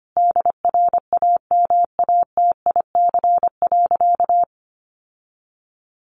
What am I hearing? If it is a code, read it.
Morse code, heard as a keyed tone, about 25 wpm, 700 Hz. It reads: DRAMATIC.
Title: Sentences from Top 1300 Words 25wpm